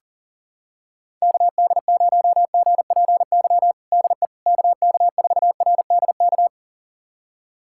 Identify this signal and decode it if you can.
Morse code, heard as a keyed tone, about 40 wpm, 700 Hz. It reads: KD0GPY DE KK4RDK
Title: USA Callsign DE Callsign 40wpm